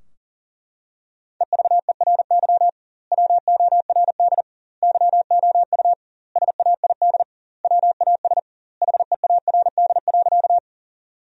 Morse code, heard as a keyed tone, about 40 wpm, 700 Hz. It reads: EVERY WORD YOU SAID WAS HEARD.
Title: 2x Word-Spacing - Sentences from Top 400 Words - 40wpm